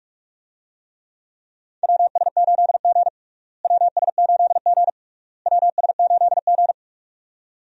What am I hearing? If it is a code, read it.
Morse code, heard as a keyed tone, about 46 wpm, 700 Hz. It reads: WS8G WS8G WS8G
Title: USA Call Signs - 4 Characters - Repeated 3x - 45wpm